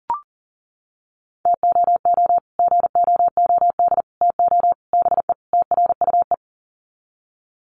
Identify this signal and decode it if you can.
Morse code, heard as a keyed tone, about 40 wpm, 700 Hz. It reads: TOO GOOD TO BE TRUE
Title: Sets of 5 Words Encore 40wpm